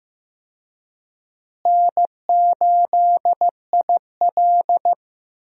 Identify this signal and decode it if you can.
Morse code, heard as a keyed tone, about 15 wpm, 700 Hz. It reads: N8IL
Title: USA Call Signs - 4 Characters - Repeated 2x - 15wpm